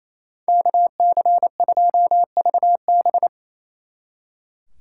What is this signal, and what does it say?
Morse code, heard as a keyed tone, about 33 wpm, 700 Hz. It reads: KC2VB
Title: Speed Racing - USA 2x2 Call Signs 28wpm